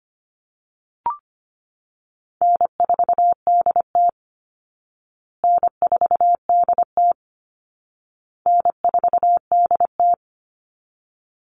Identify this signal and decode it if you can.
Morse code, heard as a keyed tone, about 25 wpm, 700 Hz. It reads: N4DT N4DT N4DT
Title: Call Signs Repeated 2x - 25wpm